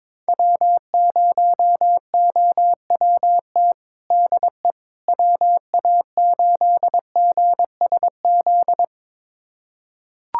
Morse code, heard as a keyed tone, about 22 wpm, 700 Hz. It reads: W0OWT DE WA8GSZ
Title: USA Callsign DE Callsign 22wpm